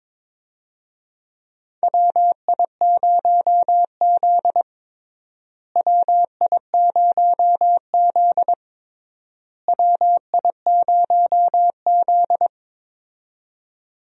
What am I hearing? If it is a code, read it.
Morse code, heard as a keyed tone, about 22 wpm, 700 Hz. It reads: WI0Z WI0Z WI0Z
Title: USA Call Signs - 4 Characters - Repeated 3x - 22wpm